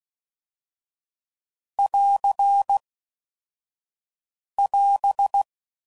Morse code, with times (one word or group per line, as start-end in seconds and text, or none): 1.8-2.8 <AR>
4.6-5.4 <AS>